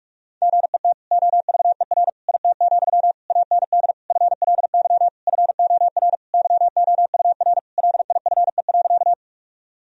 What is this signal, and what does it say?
Morse code, heard as a keyed tone, about 41 wpm, 700 Hz. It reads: GET OVER IT, AND FLY FOR YOUR LIFE.
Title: Sentences from Top 1000 Words - Review Part 2 - 45wpm